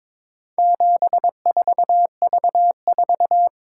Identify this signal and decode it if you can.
Morse code, heard as a keyed tone, about 22 wpm, 700 Hz. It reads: 74V4
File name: practice / 22wpm / Four Letter-Number Combinations - 22wpm